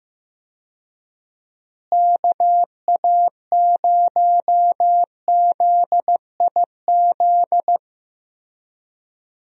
Morse code, heard as a keyed tone, about 15 wpm, 700 Hz. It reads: KA0ZIZ
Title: USA Call Signs - 6 Characters - Repeated 3x - 15wpm